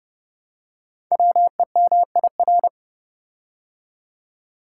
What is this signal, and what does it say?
Morse code, heard as a keyed tone, about 30 wpm, 700 Hz. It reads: WEMIR
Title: Code Groups - 5 Letters - 30wpm